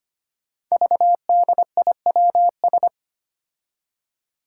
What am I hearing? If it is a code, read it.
Morse code, heard as a keyed tone, about 25 wpm, 700 Hz. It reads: VDIWS
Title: Five Challenging Letter-Number Combinations 25wpm